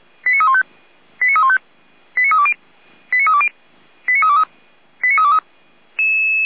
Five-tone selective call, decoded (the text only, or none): 89315 89315 89320 89320 89323 89323